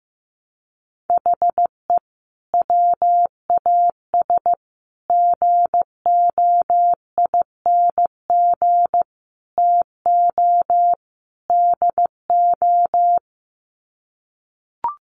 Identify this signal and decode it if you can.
Morse code, heard as a keyed tone, about 15 wpm, 700 Hz. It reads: HE WAS GOING TO DO
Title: Sets of 5 Words Encore2 15wpm